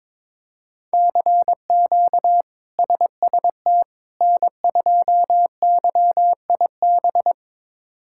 Morse code, heard as a keyed tone, about 22 wpm, 700 Hz. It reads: CQ SST N2YIB
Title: K1USN SST QSO Practice - 22wpm